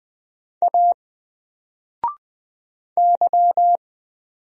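Morse code, heard as a keyed tone, about 20 wpm, 700 Hz. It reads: A Y
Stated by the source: Single letter and number Morse drill at 20 wpm, from N4LCD.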